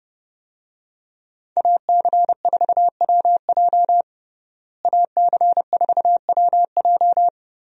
Morse code, heard as a keyed tone, about 32 wpm, 700 Hz. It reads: AC4WJ AC4WJ
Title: USA Call Signs - 5 Characters - Repeated 2x - 30wpm